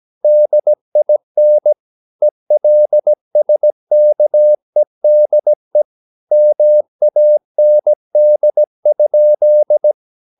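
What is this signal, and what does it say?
Morse code, heard as a keyed tone, about 17 wpm, 600 Hz. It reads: DIN ELSKEDE MAND?
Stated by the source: Slow (QRS) Morse practice at 17 wpm, from CW med Gnister.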